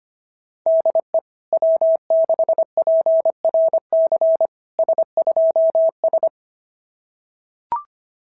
Morse code, heard as a keyed tone, about 25 wpm, 660 Hz. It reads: DE W6PRC S2S
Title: SOTA QSO Practice 25wpm